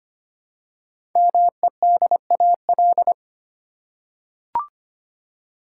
Morse code, heard as a keyed tone, about 25 wpm, 700 Hz. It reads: MEDAL
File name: 5 Letter Words - First 3 Letters Given - 25wpm